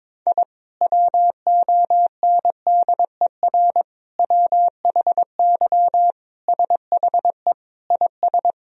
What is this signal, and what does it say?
Morse code, heard as a keyed tone, about 22 wpm, 700 Hz. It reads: I WONDER WHY SHE IS
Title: Sentences from Top 1000 Words - Review Part 3 - 22wpm